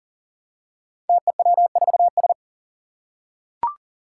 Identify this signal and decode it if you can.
Morse code, heard as a keyed tone, about 40 wpm, 700 Hz. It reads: TEW4S
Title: Five Challenging Letter-Number Combinations 40wpm